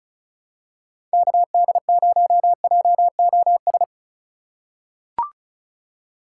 Morse code, heard as a keyed tone, about 36 wpm, 700 Hz. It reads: KD0JOS
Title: USA Call Signs - 6 Characters - 3x Character Spacing - 35wpm